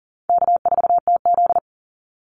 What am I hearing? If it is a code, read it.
Morse code, heard as a keyed tone, about 40 wpm, 700 Hz. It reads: K4TZ
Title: Easy Call Signs - 40wpm